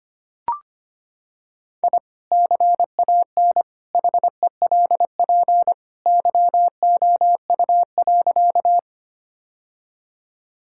Morse code, heard as a keyed tone, about 25 wpm, 700 Hz. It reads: I CAN HELP YOU.
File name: practice / 25wpm / Sentences with 2-Word Phrases - 25wpm